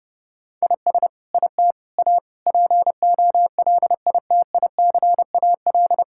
Morse code, heard as a keyed tone, about 28 wpm, 700 Hz. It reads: IS IT A POLITICAL
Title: Sentences from Top 1000 Words - Review Part 6 - 30wpm